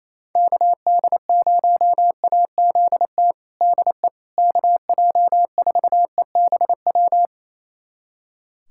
Morse code, heard as a keyed tone, about 28 wpm, 700 Hz. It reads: KD0AZT DE KJ4EBW
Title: USA Callsign DE Callsign 28wpm